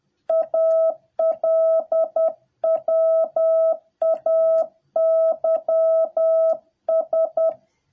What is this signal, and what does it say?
Morse code, heard as a keyed tone, about 10 wpm, 650 Hz. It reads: ALWAYS